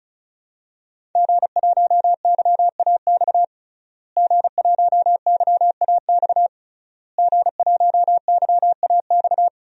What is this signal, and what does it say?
Morse code, heard as a keyed tone, about 34 wpm, 700 Hz. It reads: G1YAX G1YAX G1YAX
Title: DXCC-100 Call Signs - 5 Characters - Repeated 3x - 35wpm